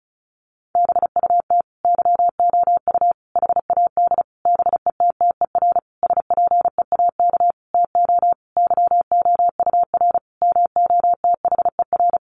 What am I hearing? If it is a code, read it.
Morse code, heard as a keyed tone, about 34 wpm, 700 Hz. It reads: BUT YOU HAD BETTER SPEAK TO YOUR MOTHER
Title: Sentences from Top 1100 Words 35wpm